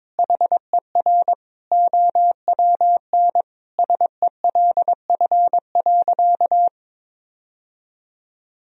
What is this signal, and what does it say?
Morse code, heard as a keyed tone, about 22 wpm, 700 Hz. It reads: HER OWN SELF.
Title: Sentences from Top 1000 Words - Review Part 1 - 22wpm